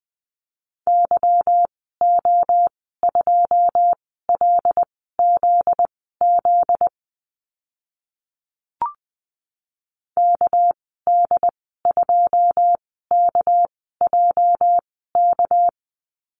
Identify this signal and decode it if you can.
Morse code, heard as a keyed tone, about 20 wpm, spaced as for 14 wpm, 700 Hz. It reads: YO2LZZ KD2KJK
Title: 245 Lesson 39g - Letter z - Call Signs 20wpm